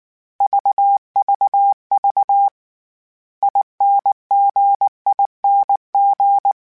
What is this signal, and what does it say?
Morse code, heard as a keyed tone, about 19 wpm, 800 Hz. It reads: VVV INGING
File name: Special INGs-mp3s / Special INGs-mp3s / Dble ings @18